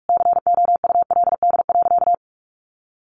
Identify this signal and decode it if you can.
Morse code, heard as a keyed tone, about 44 wpm, 700 Hz. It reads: COULD.
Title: Sentences from Top 500 Words 45wpm